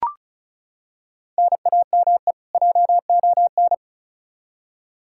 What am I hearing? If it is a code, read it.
Morse code, heard as a keyed tone, about 34 wpm, 700 Hz. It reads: NAME JON
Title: K1USN SST QSO Practice 35wpm